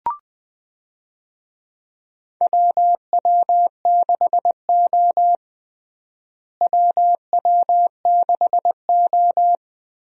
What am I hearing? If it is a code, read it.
Morse code, heard as a keyed tone, about 20 wpm, 700 Hz. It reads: WW6O WW6O
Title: USA Call Signs - 4 Characters - Repeated 3x - 20wpm